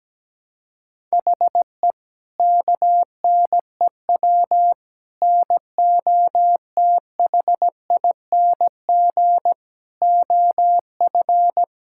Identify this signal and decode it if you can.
Morse code, heard as a keyed tone, about 17 wpm, 700 Hz. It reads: HE KNEW NOTHING OF
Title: Sentences from Top 1600 Words 17wpm